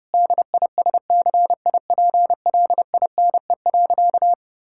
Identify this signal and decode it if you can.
Morse code, heard as a keyed tone, about 30 wpm, 700 Hz. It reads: DISCIPLINE.
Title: Sentences from Top 1000 Words - Review Part 6 - 30wpm